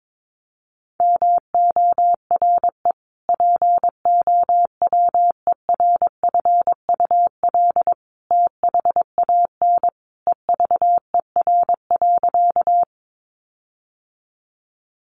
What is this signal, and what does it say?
Morse code, heard as a keyed tone, about 22 wpm, 700 Hz. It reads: MORE POWERFUL THAN EVER.
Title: Sentences from Top 1300 Words 22wpm